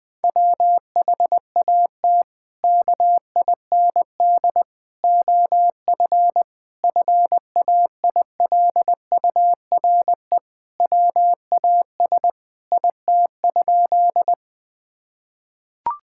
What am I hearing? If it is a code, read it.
Morse code, heard as a keyed tone, about 20 wpm, 700 Hz. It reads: WHAT KIND OF FAILURE WAS IT?
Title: Sentences from Top 1000 Words - Review Part 6 - 20wpm